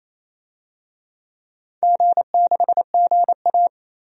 Morse code, heard as a keyed tone, about 27 wpm, 700 Hz. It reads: G6GA
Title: DXCC-100 Call Signs - 4 Characters - 28wpm